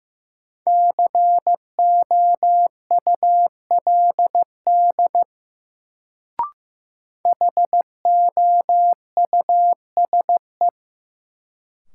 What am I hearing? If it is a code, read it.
Morse code, heard as a keyed tone, about 15 wpm, 700 Hz. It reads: COULD HOUSE
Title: Sets of 2 Words - New Word Drill - 15wpm